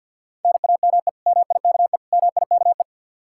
Morse code, heard as a keyed tone, about 51 wpm, 700 Hz. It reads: NAME MIKE MIKE
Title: Practice QSO - Exchange 1 - Beginning - 50wpm